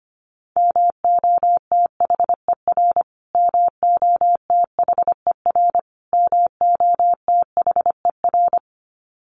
Morse code, heard as a keyed tone, about 25 wpm, 700 Hz. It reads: MOTHER MOTHER MOTHER
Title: Top 200 Words Repeated 25wpm